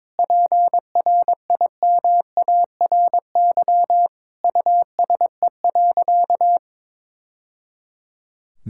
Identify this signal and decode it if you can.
Morse code, heard as a keyed tone, about 21 wpm, 700 Hz. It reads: PRIMARY USE.
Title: Sentences from Top 1000 Words - Review Part 5 - 22wpm